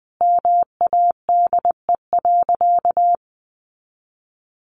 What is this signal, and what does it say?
Morse code, heard as a keyed tone, about 20 wpm, 700 Hz. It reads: MADE.
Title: Sentences from Top 1200 Words 20wpm